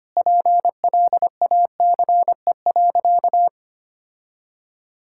Morse code, heard as a keyed tone, about 25 wpm, 700 Hz. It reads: PLACE.
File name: practice / 25wpm / Sentences with 2-Word Phrases - 25wpm